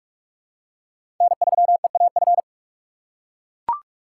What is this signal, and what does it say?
Morse code, heard as a keyed tone, about 41 wpm, 700 Hz. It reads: N3EAF
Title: USA Call Signs - 5 Characters - 4x Character Spacing - 45wpm